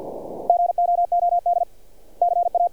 Morse code, heard as a keyed tone, about 46 wpm, 700 Hz. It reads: GOOD CA
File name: archived / CRSnet